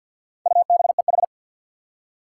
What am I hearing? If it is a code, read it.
Morse code, heard as a keyed tone, about 49 wpm, 700 Hz. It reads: UBEH
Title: Four Challenging Letter-Number Combinations 50wpm